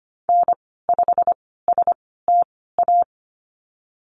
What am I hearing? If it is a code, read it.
Morse code, heard as a keyed tone, about 25 wpm, spaced as for 16 wpm, 700 Hz. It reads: N5STA
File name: Lesson 15g - Number 5 - Call Signs 25wpm